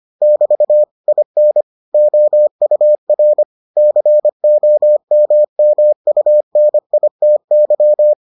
Morse code, heard as a keyed tone, about 25 wpm, 600 Hz. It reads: <BT> IN OUR COMMUNITY